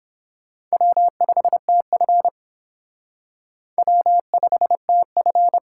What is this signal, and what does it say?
Morse code, heard as a keyed tone, about 26 wpm, 700 Hz. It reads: W5TF W5TF
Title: Speed Racing - USA Call Signs 22wpm